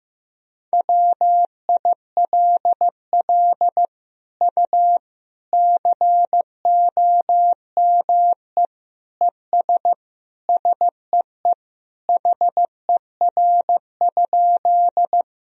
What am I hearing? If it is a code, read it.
Morse code, heard as a keyed tone, about 15 wpm, 700 Hz. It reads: WILL U COME ES SEE HER?
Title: Sentences from Top 100 Words and Abbreviations 15wpm